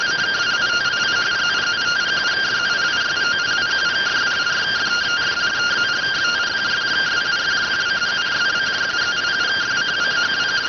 Two-tone teleprinter signal, about 170 Hz shift.